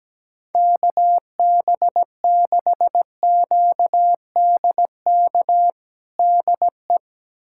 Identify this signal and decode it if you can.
Morse code, heard as a keyed tone, about 17 wpm, 700 Hz. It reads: KB6QDK DE
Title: USA Callsign DE Callsign 17wpm